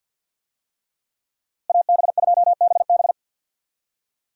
Morse code, heard as a keyed tone, about 49 wpm, 700 Hz. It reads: AB2BB
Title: USA Call Signs - 5 Characters - Repeated 3x - 50wpm